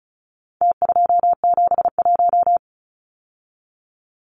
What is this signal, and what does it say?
Morse code, heard as a keyed tone, about 34 wpm, 700 Hz. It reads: T271
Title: Numbers - Contest Style - 4-Digit 35wpm